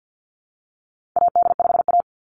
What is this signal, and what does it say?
Morse code, heard as a keyed tone, about 50 wpm, 700 Hz. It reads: AD5A
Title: Call Signs Repeated 2x - 50wpm